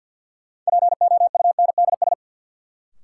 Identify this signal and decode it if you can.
Morse code, heard as a keyed tone, about 47 wpm, 700 Hz. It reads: POUNDS
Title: Sentences from Top 1000 Words - Review Part 2 - New Word Drill - 50wpm